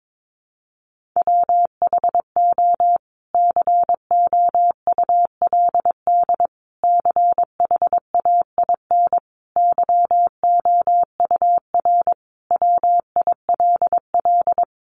Morse code, heard as a keyed tone, about 22 wpm, 700 Hz. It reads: WHO COULD CHAIN YOUR WILL
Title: Sentences from Top 1500 Words 22wpm